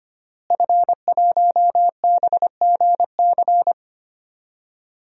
Morse code, heard as a keyed tone, about 25 wpm, 700 Hz. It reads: F1BGC
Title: DXCC-100 Call Signs - 5 Characters - Repeated 3x - 25wpm